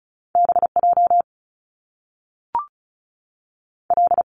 Morse code, heard as a keyed tone, about 36 wpm, 700 Hz. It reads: B J L
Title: Top Two Letter Combinations in Words 35wpm